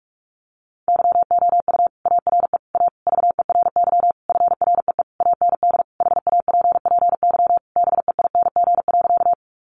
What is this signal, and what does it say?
Morse code, heard as a keyed tone, about 44 wpm, 700 Hz. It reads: YOU ARE A VERY FREE AND HAPPY BEING.
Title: Sentences from Top 1000 Words 45wpm